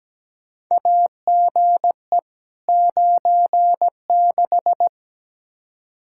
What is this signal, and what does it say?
Morse code, heard as a keyed tone, about 17 wpm, 700 Hz. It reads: AGE 96
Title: Practice QSO - Exchange 3 - More Info - 17wpm